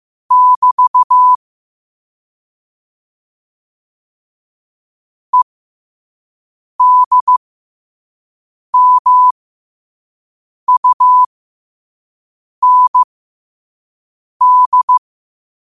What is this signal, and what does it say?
Morse code, heard as a keyed tone, about 15 wpm, spaced as for 5 wpm, 1000 Hz. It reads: <BT> EDMUND
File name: qotdfiles / 5wpm